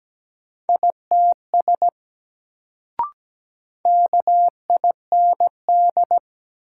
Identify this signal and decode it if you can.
Morse code, heard as a keyed tone, about 17 wpm, 700 Hz. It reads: ITS KIND
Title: Sets of 2 Words - New Word Drill - 17wpm